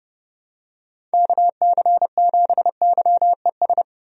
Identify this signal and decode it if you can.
Morse code, heard as a keyed tone, about 30 wpm, 700 Hz. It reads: KC7YES
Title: USA Call Signs - 6 Characters - 30wpm